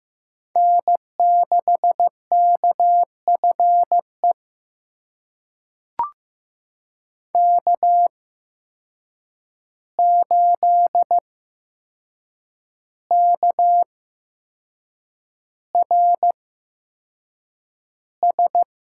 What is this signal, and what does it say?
Morse code, heard as a keyed tone, about 15 wpm, spaced as for 6 wpm, 700 Hz. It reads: N6KFE K8KRS
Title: USA Call Signs - 5 Characters - 8x Character Spacing - 15wpm